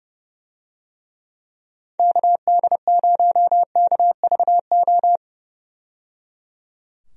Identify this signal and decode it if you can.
Morse code, heard as a keyed tone, about 30 wpm, 700 Hz. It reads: KD0KVO
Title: USA Call Signs - 6 Characters - 30wpm